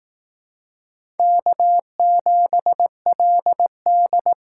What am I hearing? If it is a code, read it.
Morse code, heard as a keyed tone, about 18 wpm, 700 Hz. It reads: K7LD
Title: Speed Racing - USA 1x2 Call Signs 15wpm